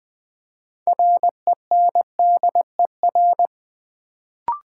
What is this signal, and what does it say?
Morse code, heard as a keyed tone, about 20 wpm, 700 Hz. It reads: RENDER
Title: Build A Word - 2 Letters At A Time - 6-Letter Words - 20wpm